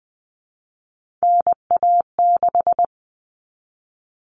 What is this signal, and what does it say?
Morse code, heard as a keyed tone, about 20 wpm, 700 Hz. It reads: NA6
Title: Common Cut Numbers - 3-Digit 20wpm